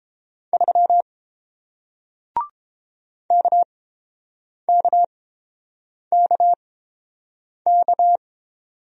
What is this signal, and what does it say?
Morse code, heard as a keyed tone, about 30 wpm, spaced as for 7 wpm, 700 Hz. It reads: 3 KKKK